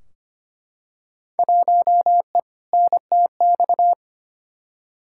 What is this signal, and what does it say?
Morse code, heard as a keyed tone, about 25 wpm, 700 Hz. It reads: 1E NTX
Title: Field Day Exchanges - 25wpm